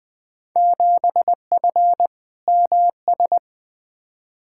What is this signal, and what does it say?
Morse code, heard as a keyed tone, about 20 wpm, 700 Hz. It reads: 7F MS
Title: Field Day Exchanges Encore - 20wpm